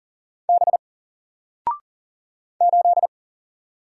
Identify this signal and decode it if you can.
Morse code, heard as a keyed tone, about 40 wpm, 700 Hz. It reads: B 8
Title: Single Character (Letter:Number:Punctuation:Prosign) Alt 40wpm